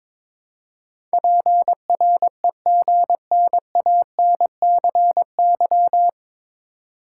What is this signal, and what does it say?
Morse code, heard as a keyed tone, about 23 wpm, 700 Hz. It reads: PREGNANCY
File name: 9 Letter Words - First 3 Letters Given - 22wpm